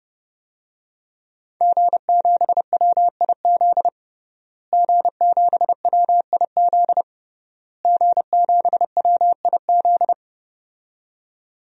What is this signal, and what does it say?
Morse code, heard as a keyed tone, about 30 wpm, 700 Hz. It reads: G7WIZ G7WIZ G7WIZ
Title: DXCC-100 Call Signs - 5 Characters - Repeated 3x - 30wpm